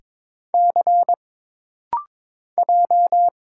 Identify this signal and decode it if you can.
Morse code, heard as a keyed tone, about 22 wpm, 700 Hz. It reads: C J
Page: This rapid-fire timing is geared to push your ICR (Instant Character Recognition) skills.